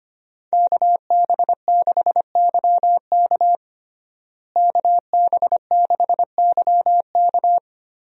Morse code, heard as a keyed tone, about 25 wpm, 700 Hz. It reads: KB6YK KB6YK
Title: USA Call Signs - 5 Characters - Repeated 3x - 25wpm